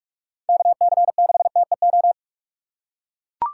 Morse code, heard as a keyed tone, about 41 wpm, 700 Hz. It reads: KC6TEO
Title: USA Call Signs - 6 Characters - 6x Character Spacing - 45wpm